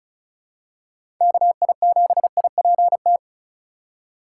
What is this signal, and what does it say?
Morse code, heard as a keyed tone, about 36 wpm, 700 Hz. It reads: KI7IPT
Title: USA Call Signs - 6 Characters - 3x Character Spacing - 35wpm